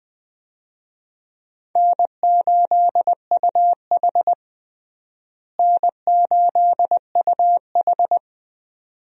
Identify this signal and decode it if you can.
Morse code, heard as a keyed tone, about 20 wpm, 700 Hz. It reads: N8UH N8UH
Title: USA Call Signs - 4 Characters - Repeated 2x - 20wpm